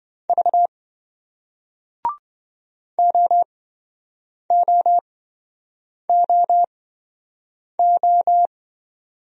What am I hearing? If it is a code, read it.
Morse code, heard as a keyed tone, about 27 wpm, spaced as for 7 wpm, 700 Hz. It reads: V OOOO